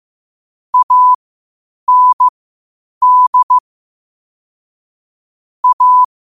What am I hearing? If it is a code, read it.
Morse code, heard as a keyed tone, about 15 wpm, spaced as for 8 wpm, 1000 Hz. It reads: AND A
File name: qotdfiles / 8wpm